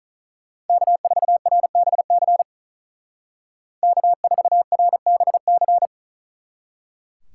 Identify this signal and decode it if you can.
Morse code, heard as a keyed tone, about 43 wpm, 700 Hz. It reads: K4RBC K4RBC
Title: Speed Racing - USA Call Signs 35wpm